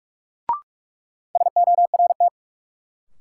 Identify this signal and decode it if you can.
Morse code, heard as a keyed tone, about 43 wpm, 700 Hz. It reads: SORT
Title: Sets of 2 Words - New Word Drill - 45wpm